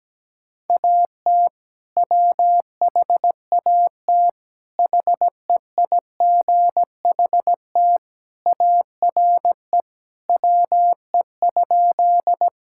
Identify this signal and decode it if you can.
Morse code, heard as a keyed tone, about 17 wpm, 700 Hz. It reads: AT WHAT HEIGHT ARE WE?
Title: Sentences from Top 1600 Words 17wpm